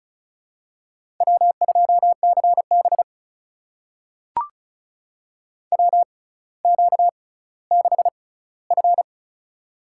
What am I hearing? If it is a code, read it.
Morse code, heard as a keyed tone, about 36 wpm, spaced as for 18 wpm, 700 Hz. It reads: W2CB WQ6F
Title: USA Call Signs - 4 Characters - 6x Character Spacing - 35wpm